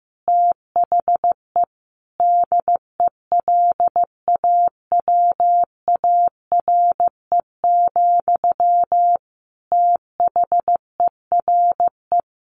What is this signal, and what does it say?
Morse code, heard as a keyed tone, about 15 wpm, 700 Hz. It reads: THE DELAWARE, THERE
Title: A Sound Of Thunder 15wpm